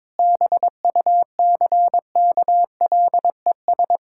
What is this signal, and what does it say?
Morse code, heard as a keyed tone, about 22 wpm, 700 Hz. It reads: BUCKLES
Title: Pangram Challenge v2 - 22wpm